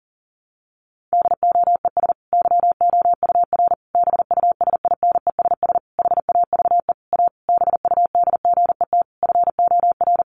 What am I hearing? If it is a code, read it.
Morse code, heard as a keyed tone, about 40 wpm, 700 Hz. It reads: DOES YOUR BUSINESS HAVE A BUDGET FOR
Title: Sentences from Top 1200 Words 40wpm